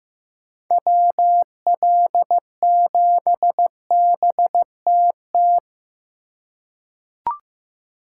Morse code, heard as a keyed tone, about 16 wpm, 700 Hz. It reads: WL7BTT
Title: USA Call Signs - 6 Characters - 3x Character Spacing - 15wpm